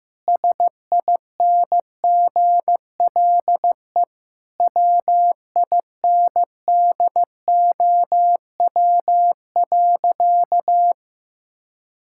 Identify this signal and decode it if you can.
Morse code, heard as a keyed tone, about 15 wpm, 700 Hz. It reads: SINGLE WINDOW.
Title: Sentences from Top 1600 Words 15wpm